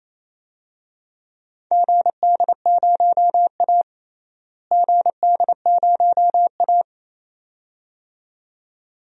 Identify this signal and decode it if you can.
Morse code, heard as a keyed tone, about 28 wpm, 700 Hz. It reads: GD0A GD0A
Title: DXCC-100 Call Signs - 4 Characters - Repeated 2x - 28wpm